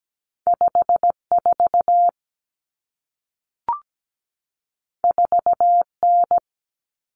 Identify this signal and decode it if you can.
Morse code, heard as a keyed tone, about 17 wpm, 700 Hz. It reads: 54 4N
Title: Common Cut Numbers - 2-Digit 17wpm